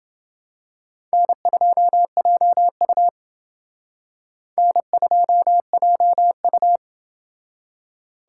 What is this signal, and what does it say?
Morse code, heard as a keyed tone, about 28 wpm, 700 Hz. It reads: N2JU N2JU